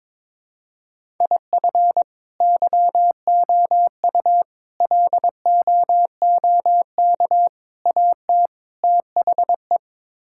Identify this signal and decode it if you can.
Morse code, heard as a keyed tone, about 22 wpm, 700 Hz. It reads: IF YOU LOOK AT THE
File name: practice / 22wpm / Build A Phrase - 5-Word Phrases - 22wpm